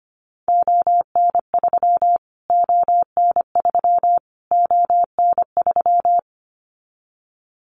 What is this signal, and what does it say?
Morse code, heard as a keyed tone, about 25 wpm, 700 Hz. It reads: ON3 ON3 ON3
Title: Callsign Prefixes for Top 25 DXCC - Repeated 3x - 25wpm